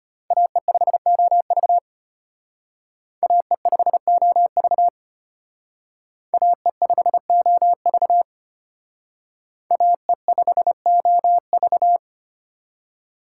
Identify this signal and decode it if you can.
Morse code, heard as a keyed tone, about 32 wpm, 700 Hz. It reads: AE5OV AE5OV AE5OV AE5OV